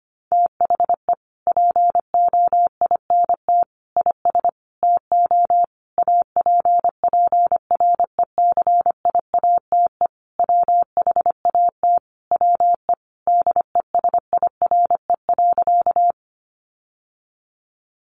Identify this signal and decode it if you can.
Morse code, heard as a keyed tone, about 25 wpm, 700 Hz. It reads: THE POINT IS TO APPRECIATE WHAT WE DESIRE.
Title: Sentences from Top 1300 Words 25wpm